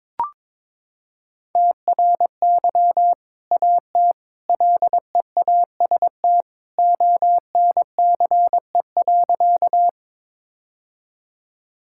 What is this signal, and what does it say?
Morse code, heard as a keyed tone, about 22 wpm, 700 Hz. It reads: TRY AT LEAST ONCE.
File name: practice / 22wpm / Sentences with 2-Word Phrases - 22wpm